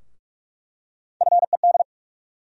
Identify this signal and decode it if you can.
Morse code, heard as a keyed tone, about 43 wpm, 700 Hz. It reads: FED
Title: USA Call Sign Postfixes - 3 Letters 45wpm